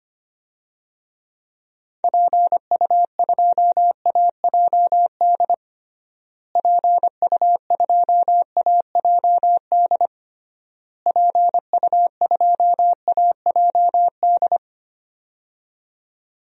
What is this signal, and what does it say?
Morse code, heard as a keyed tone, about 25 wpm, 700 Hz. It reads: PU2AJD PU2AJD PU2AJD
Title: DXCC-100 Call Signs - 6 Characters - Repeated 3x - 25wpm